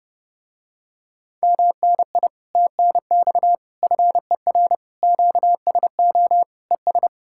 Morse code, heard as a keyed tone, about 30 wpm, 700 Hz. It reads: MNI TNX FER QSO ES